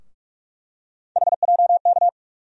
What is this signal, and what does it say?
Morse code, heard as a keyed tone, about 46 wpm, 700 Hz. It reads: HJK
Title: USA Call Sign Postfixes - 3 Letters 45wpm